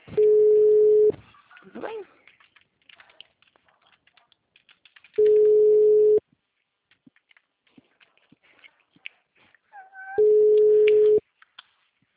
Ringback tone (Europe, 425 Hz).